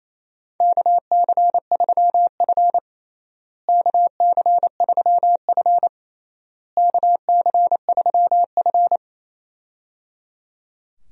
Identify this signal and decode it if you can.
Morse code, heard as a keyed tone, about 28 wpm, 700 Hz. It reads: KC3F KC3F KC3F
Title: USA Call Signs - 4 Characters - Repeated 3x - 28wpm